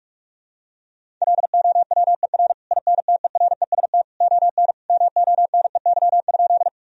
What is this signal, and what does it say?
Morse code, heard as a keyed tone, about 46 wpm, 700 Hz. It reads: LOWER INTEREST ON MONEY?
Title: Sentences from Top 1000 Words - Review Part 4 - 45wpm